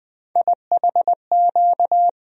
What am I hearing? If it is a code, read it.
Morse code, heard as a keyed tone, about 20 wpm, 700 Hz. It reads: IHQ
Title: USA Call Sign Postfixes - 3 Letters 20wpm